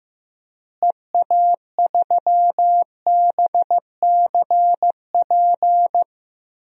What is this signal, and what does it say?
Morse code, heard as a keyed tone, about 15 wpm, 700 Hz. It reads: EA3BCP
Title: DXCC-100 Call Signs - 6 Characters - Repeated 3x - 15wpm